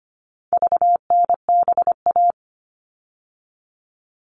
Morse code, heard as a keyed tone, about 25 wpm, 700 Hz. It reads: VNBA
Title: All Cut Numbers - 4-Digit 25wpm